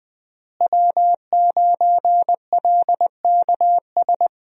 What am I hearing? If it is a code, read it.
Morse code, heard as a keyed tone, about 20 wpm, 700 Hz. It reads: W9LKS
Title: USA Call Signs - 5 Characters - Repeated 2x - 20wpm